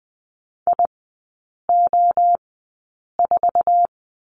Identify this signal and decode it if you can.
Morse code, heard as a keyed tone, about 20 wpm, spaced as for 8 wpm, 700 Hz. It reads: IO4
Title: 047 Lesson 9e - Number 4 - Three Characters 20wpm